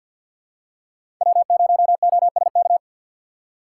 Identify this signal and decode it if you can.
Morse code, heard as a keyed tone, about 53 wpm, 700 Hz. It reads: W0OSK
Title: USA Call Signs - 5 Characters - 2x Character Spacing - 50wpm